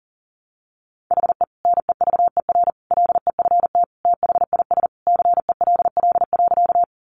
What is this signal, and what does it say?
Morse code, heard as a keyed tone, about 40 wpm, 700 Hz. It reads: HE NEVER LEFT THIS CELL.
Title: Sentences from Top 1100 Words 40wpm